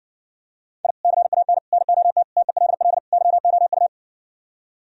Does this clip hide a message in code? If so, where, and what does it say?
Yes: Morse code from 0.8 to 3.9 s, heard as a keyed tone, about 68 wpm, 700 Hz. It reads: I CAN NOT TELL YOU